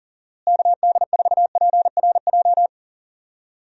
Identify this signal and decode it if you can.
Morse code, heard as a keyed tone, about 39 wpm, 700 Hz. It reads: KD4PRJ
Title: USA Call Signs - 6 Characters - Repeated 2x - 40wpm